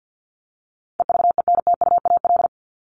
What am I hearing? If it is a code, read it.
Morse code, heard as a keyed tone, about 48 wpm, 700 Hz. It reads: EVENTUAL
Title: LY Suffix - Top 100 Words with Base Words 50wpm